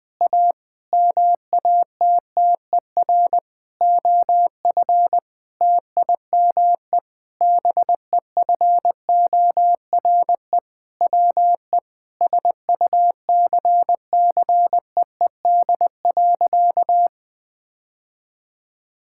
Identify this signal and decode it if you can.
Morse code, heard as a keyed tone, about 20 wpm, 700 Hz. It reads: A MATTER OF TIME BEFORE WE SUCCEED.
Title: Sentences with 6-Word Phrases - 20wpm